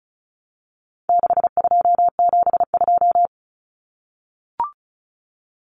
Morse code, heard as a keyed tone, about 34 wpm, 700 Hz. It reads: 6272
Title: Numbers - 4-Digit 35wpm